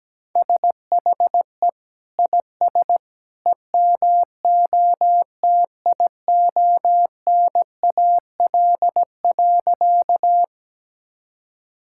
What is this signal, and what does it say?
Morse code, heard as a keyed tone, about 17 wpm, 700 Hz. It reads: SHE IS EMOTIONAL.
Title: Sentences from Top 1000 Words - Review Part 5 - 17wpm